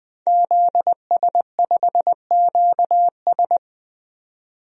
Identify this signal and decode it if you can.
Morse code, heard as a keyed tone, about 20 wpm, 700 Hz. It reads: ZS5QS
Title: DXCC-100 Call Signs - 5 Characters - 20wpm